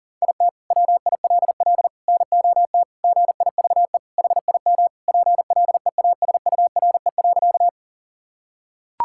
Morse code, heard as a keyed tone, about 40 wpm, 700 Hz. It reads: IT WILL NOT GIVE HIM PLEASURE.
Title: Sentences from Top 1600 Words 40wpm